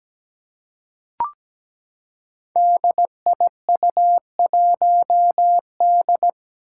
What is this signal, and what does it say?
Morse code, heard as a keyed tone, about 17 wpm, 700 Hz. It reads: DIU1D
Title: Five Challenging Letter-Number Combinations 17wpm